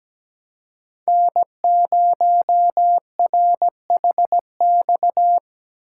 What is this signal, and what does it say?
Morse code, heard as a keyed tone, about 17 wpm, 700 Hz. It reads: N0RHX
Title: USA Call Signs - 5 Characters - 2x Character Spacing - 17wpm